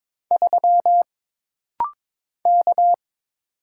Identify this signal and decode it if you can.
Morse code, heard as a keyed tone, about 22 wpm, 700 Hz. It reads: 3 K
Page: This practice set is composed of Letters, Numbers, common punctuations, and common prosigns.